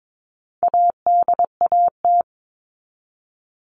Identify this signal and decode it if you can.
Morse code, heard as a keyed tone, about 22 wpm, 700 Hz. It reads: ADAT
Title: All Cut Numbers - 4-Digit 22wpm